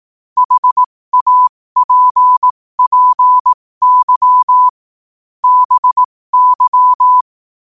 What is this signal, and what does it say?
Morse code, heard as a keyed tone, about 18 wpm, spaced as for 15 wpm, 1000 Hz. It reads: HAPPY BY